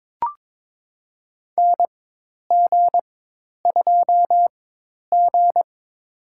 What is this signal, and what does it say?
Morse code, heard as a keyed tone, about 21 wpm, spaced as for 10 wpm, 700 Hz. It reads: NG2G
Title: USA Call Signs - 4 Characters - 4x Character Spacing - 22wpm